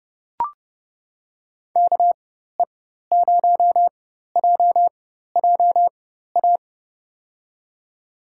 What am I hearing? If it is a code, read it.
Morse code, heard as a keyed tone, about 30 wpm, spaced as for 14 wpm, 700 Hz. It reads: KE0JJA
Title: USA Call Signs - 6 Characters - 4x Character Spacing - 30wpm